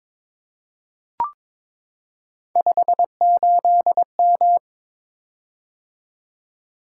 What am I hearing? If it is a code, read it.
Morse code, heard as a keyed tone, about 22 wpm, 700 Hz. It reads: 58M
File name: practice / 22wpm / Three Letter-Number Combinations - 22wpm